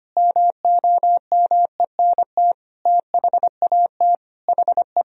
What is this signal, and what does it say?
Morse code, heard as a keyed tone, about 25 wpm, 700 Hz. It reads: MOMENT THAT HE
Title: Sentences from Top 1000 Words - Review Part 3 - 25wpm